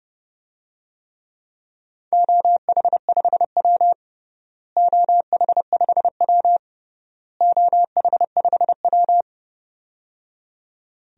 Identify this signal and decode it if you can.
Morse code, heard as a keyed tone, about 30 wpm, 700 Hz. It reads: OH5W OH5W OH5W
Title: DXCC-100 Call Signs - 4 Characters - Repeated 3x - 30wpm